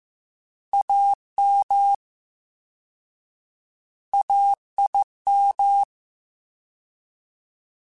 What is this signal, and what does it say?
Morse code, heard as a keyed tone, about 15 wpm, 800 Hz. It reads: AM AIM
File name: Alphabetical / Alphabetical mp3s / As @13